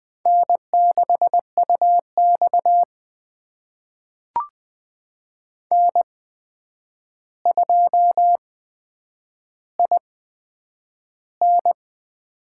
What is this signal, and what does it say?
Morse code, heard as a keyed tone, about 20 wpm, spaced as for 9 wpm, 700 Hz. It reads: N6UX N2IN
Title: USA Call Signs - 8x Character Spacing - 20wpm